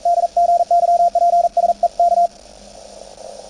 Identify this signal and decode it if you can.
Morse code, heard as a keyed tone, about 41 wpm, 690 Hz, with radio noise. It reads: DGYPSE K
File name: attachments / CW 20m 200Hz